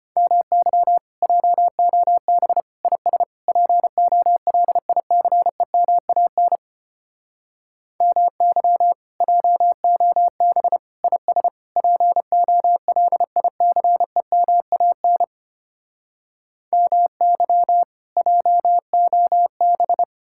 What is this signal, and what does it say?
Morse code, heard as a keyed tone, about 33 wpm, 700 Hz. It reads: MY JOB IS POLICEMAN MY JOB IS POLICEMAN MY JOB